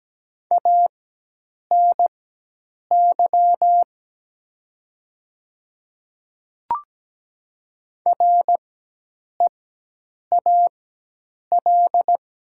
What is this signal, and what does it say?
Morse code, heard as a keyed tone, about 17 wpm, spaced as for 8 wpm, 700 Hz. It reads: ANY REAL
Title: 4x Character Spacing - Top 500 Words - 17wpm